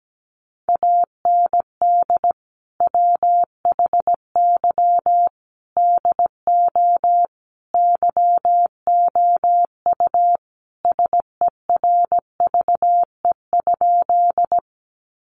Morse code, heard as a keyed tone, about 17 wpm, 700 Hz. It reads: AND WHY DO YOU SERVE?
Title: Sentences from Top 400 Words 17wpm